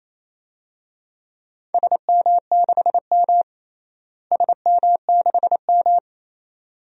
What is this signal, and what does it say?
Morse code, heard as a keyed tone, about 28 wpm, 700 Hz. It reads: SM6M SM6M
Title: DXCC-100 Call Signs - 4 Characters - Repeated 2x - 28wpm